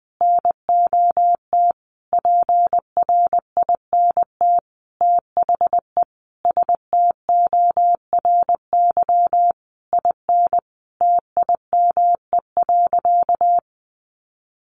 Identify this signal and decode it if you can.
Morse code, heard as a keyed tone, about 20 wpm, 700 Hz. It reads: NOT PRINT THE STORY IN TIME.
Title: Sentences from Top 1400 Words 20wpm